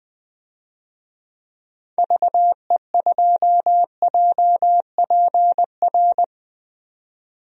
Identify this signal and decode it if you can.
Morse code, heard as a keyed tone, about 20 wpm, 700 Hz. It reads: VE2JPR
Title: DXCC-100 Call Signs - 6 Characters - 20wpm